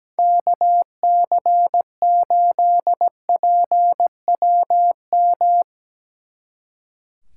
Morse code, heard as a keyed tone, about 18 wpm, 700 Hz. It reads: KC8PWM
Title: Speed Racing - USA 2x3 Call Signs 17wpm